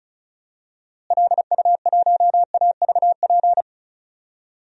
Morse code, heard as a keyed tone, about 36 wpm, 700 Hz. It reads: LU1AVP
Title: DXCC-100 Call Signs - 6 Characters - Repeated 3x - 35wpm